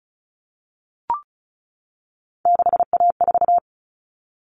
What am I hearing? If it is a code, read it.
Morse code, heard as a keyed tone, about 34 wpm, 700 Hz. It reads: 6A4
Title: Common Cut Numbers - 3-Digit 35wpm